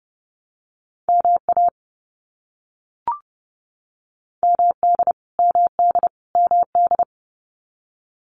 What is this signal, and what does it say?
Morse code, heard as a keyed tone, about 30 wpm, 700 Hz. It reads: MA MD MD MD
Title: US States Repeated 3x 30wpm